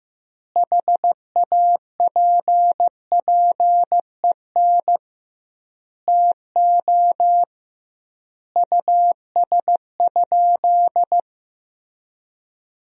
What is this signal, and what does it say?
Morse code, heard as a keyed tone, about 15 wpm, 700 Hz. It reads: HAPPEN TO US?
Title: 2x Word-Spacing - Sentences from Top 400 Words - 15wpm